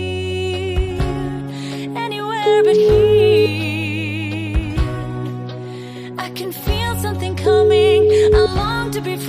Ringback tone (Europe, 425 Hz), with noise.